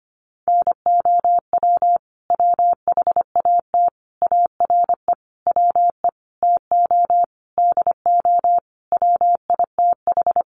Morse code, heard as a keyed tone, about 25 wpm, 700 Hz. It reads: NOW WHAT ARE WE TO DO WITH
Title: Sentences from Top 100 Words 25wpm